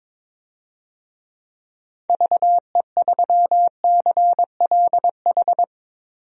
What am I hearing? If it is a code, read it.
Morse code, heard as a keyed tone, about 21 wpm, 700 Hz. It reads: VE3CLH
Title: DXCC-100 Call Signs - 6 Characters - 22wpm